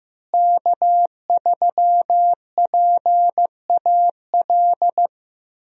Morse code, heard as a keyed tone, about 16 wpm, 700 Hz. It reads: K3PAL
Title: USA Call Signs - 5 Characters - Repeated 2x - 15wpm